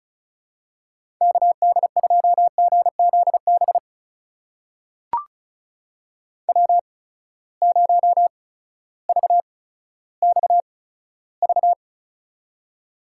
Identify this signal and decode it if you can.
Morse code, heard as a keyed tone, about 36 wpm, spaced as for 16 wpm, 700 Hz. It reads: KD2GZB W0VXV
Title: USA Call Signs - 8x Character Spacing - 35wpm